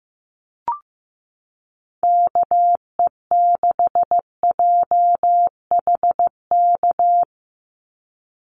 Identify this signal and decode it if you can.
Morse code, heard as a keyed tone, about 15 wpm, 700 Hz. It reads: KE6JHK
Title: USA Call Signs 15wpm